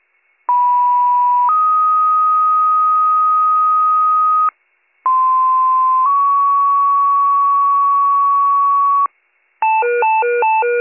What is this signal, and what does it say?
Two-tone page, heard as a one-second tone, then a three-second tone, about 980 then 1250 Hz, about 1040 then 1120 Hz. It is heard from 0.5 to 9.1 s.